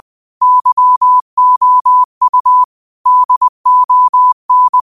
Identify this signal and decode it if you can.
Morse code, heard as a keyed tone, about 20 wpm, 1000 Hz. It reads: YOU DON
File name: qotdfiles / 20wpm